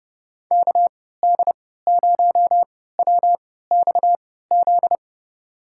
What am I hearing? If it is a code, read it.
Morse code, heard as a keyed tone, about 30 wpm, spaced as for 17 wpm, 700 Hz. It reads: KD0WXZ
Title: USA Call Signs - 6 Characters - 3x Character Spacing - 30wpm